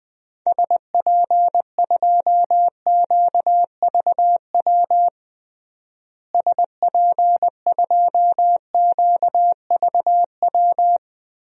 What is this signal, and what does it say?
Morse code, heard as a keyed tone, about 21 wpm, 700 Hz. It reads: SP2QVW SP2QVW
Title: DXCC-100 Call Signs - 6 Characters - Repeated 3x - 20wpm